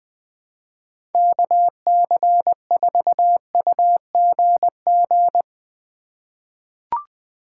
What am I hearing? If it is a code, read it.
Morse code, heard as a keyed tone, about 21 wpm, 700 Hz. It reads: KC4UGG
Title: USA Call Signs - 6 Characters - 2x Character Spacing - 20wpm